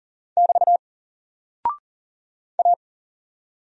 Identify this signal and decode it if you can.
Morse code, heard as a keyed tone, about 41 wpm, 700 Hz. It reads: <BT> A
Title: Single Character (Letter:Number:Punctuation:Prosign) 40wpm